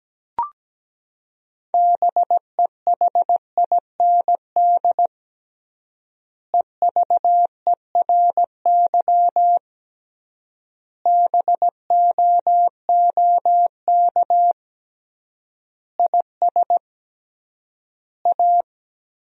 Morse code, heard as a keyed tone, about 17 wpm, 700 Hz. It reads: BEHIND EVERY BOOK IS A
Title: 3x Word-Spacing - Sentences from Top 500 Words - 17wpm